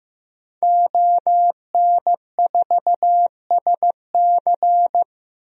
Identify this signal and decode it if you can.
Morse code, heard as a keyed tone, about 15 wpm, 700 Hz. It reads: ON4SC
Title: DXCC-100 Call Signs - 5 Characters - 15wpm